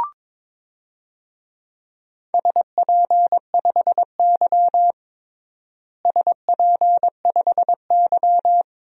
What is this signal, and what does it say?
Morse code, heard as a keyed tone, about 22 wpm, 700 Hz. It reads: SP5Y SP5Y
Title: DXCC-100 Call Signs - 4 Characters - Repeated 2x - 22wpm